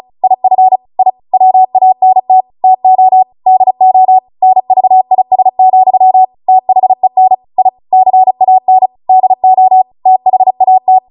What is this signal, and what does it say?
Morse code, heard as a keyed tone, about 36 wpm, 750 Hz. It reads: IF I WANT TO DO NVIS, THEN I CAN DO THAT
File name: Morse / 35WPM